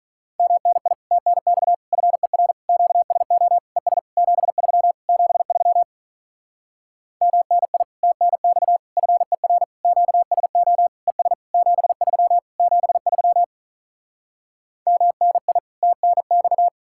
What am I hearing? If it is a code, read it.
Morse code, heard as a keyed tone, about 48 wpm, 700 Hz. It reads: MNI TNX FER QSO ES 73 73 MNI TNX FER QSO ES 73 73 MNI TNX